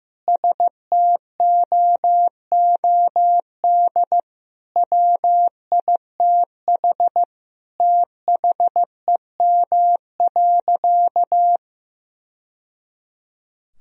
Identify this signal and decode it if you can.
Morse code, heard as a keyed tone, about 16 wpm, 700 Hz. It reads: STOOD WITH THEM.
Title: Sentences from Top 1000 Words - Review Part 2 - 15wpm